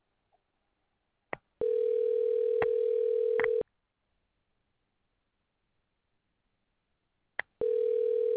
Ringback tone (North America), 440 + 480 Hz.